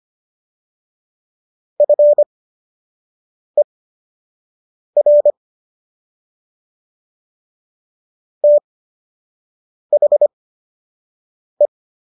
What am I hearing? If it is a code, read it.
Morse code, heard as a keyed tone, about 25 wpm, spaced as for 6 wpm, 590 Hz. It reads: FER THE